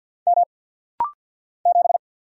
Morse code, heard as a keyed tone, about 50 wpm, 690 Hz. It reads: M 7
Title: Single Letter-Number Alt - Rapid-Fire 50wpm